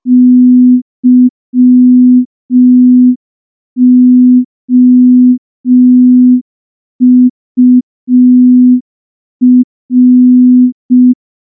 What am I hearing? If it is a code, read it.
Morse code, heard as a keyed tone, about 5 wpm, 250 Hz. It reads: YOUR
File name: morse / morse-training-05WPM